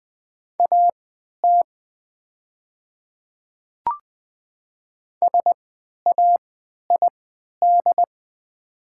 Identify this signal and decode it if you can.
Morse code, heard as a keyed tone, about 20 wpm, spaced as for 11 wpm, 700 Hz. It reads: AT SAID
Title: 3x Character Spacing - Top 300 Words - 20wpm